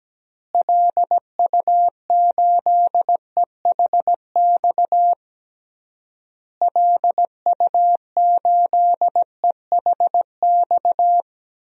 Morse code, heard as a keyed tone, about 17 wpm, 700 Hz. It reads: LU8EHX LU8EHX
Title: DXCC-100 Call Signs - 6 Characters - Repeated 3x - 17wpm